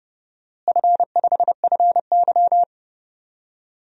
Morse code, heard as a keyed tone, about 30 wpm, 700 Hz. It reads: F5FY
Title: DXCC-100 Call Signs - 4 Characters - Repeated 2x - 30wpm